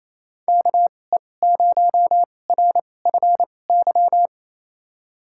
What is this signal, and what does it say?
Morse code, heard as a keyed tone, about 28 wpm, spaced as for 20 wpm, 700 Hz. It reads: KE0RFY
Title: USA Call Signs - 6 Characters - 2x Character Spacing - 28wpm